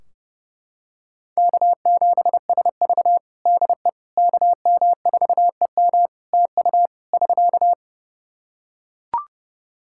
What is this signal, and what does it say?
Morse code, heard as a keyed tone, about 30 wpm, 700 Hz. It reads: K7SV DE KM4EM TU <SK>
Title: Practice QSO - Exchange 4 - Ending - 30wpm